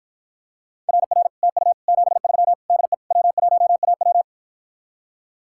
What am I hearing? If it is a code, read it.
Morse code, heard as a keyed tone, about 53 wpm, 700 Hz. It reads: RR TU 73 DE W1AW